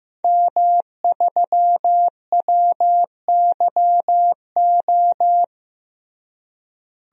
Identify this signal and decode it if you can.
Morse code, heard as a keyed tone, about 15 wpm, 700 Hz. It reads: M3WYO
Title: DXCC-100 Call Signs - 5 Characters - Repeated 3x - 15wpm